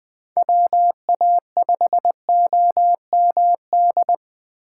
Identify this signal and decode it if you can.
Morse code, heard as a keyed tone, about 20 wpm, 700 Hz. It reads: WA5OMD
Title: USA Call Signs - 3x Character Spacing - 20wpm